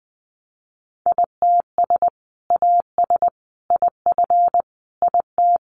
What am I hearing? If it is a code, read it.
Morse code, heard as a keyed tone, about 19 wpm, 700 Hz. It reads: ITS AS IF IT
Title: Sentences from Top 800 Words 20wpm